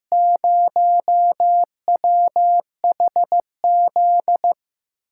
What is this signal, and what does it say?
Morse code, heard as a keyed tone, about 15 wpm, 700 Hz. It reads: 0WHZ
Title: USA Call Signs - 8x Character Spacing - 15wpm